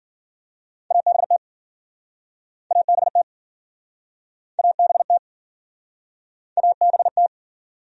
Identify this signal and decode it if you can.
Morse code, heard as a keyed tone, about 57 wpm, 700 Hz. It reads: ABT ABT ABT ABT